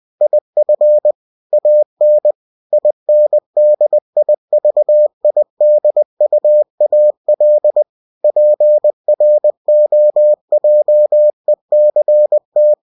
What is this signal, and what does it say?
Morse code, heard as a keyed tone, about 20 wpm, 600 Hz. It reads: IF AN INDIVIDUAL PROJECT
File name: cw-books / winnie-the-pooh / winnie-20wpm-page-0115